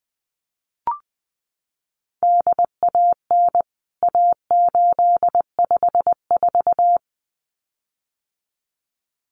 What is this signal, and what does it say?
Morse code, heard as a keyed tone, about 20 wpm, 700 Hz. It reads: DAN A854
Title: CWT Exchange 20wpm